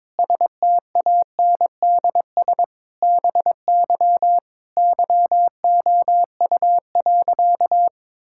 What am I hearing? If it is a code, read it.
Morse code, heard as a keyed tone, about 22 wpm, 700 Hz. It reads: STANDS BY YOU.
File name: Difficult Sentences with Hints - 22wpm